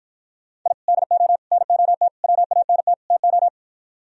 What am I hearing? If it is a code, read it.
Morse code, heard as a keyed tone, about 55 wpm, 700 Hz. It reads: I DO NOT WANT TO